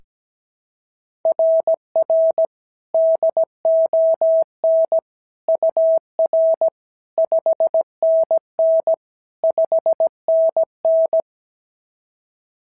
Morse code, heard as a keyed tone, about 16 wpm, 650 Hz. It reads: RR DON UR 5NN 5NN
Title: SOTA QSO Practice 17wpm